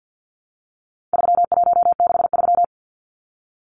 Morse code, heard as a keyed tone, about 50 wpm, 700 Hz. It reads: 3163
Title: Numbers - 4-Digit 50wpm